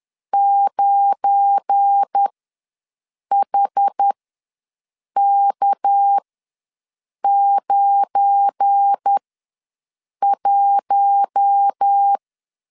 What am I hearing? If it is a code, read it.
Morse code, heard as a keyed tone, about 10 wpm, spaced as for 6 wpm, 800 Hz. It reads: 9HK91
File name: LEZIONI / 7 GF9Z / 7 GF9 ws 10wpm 4034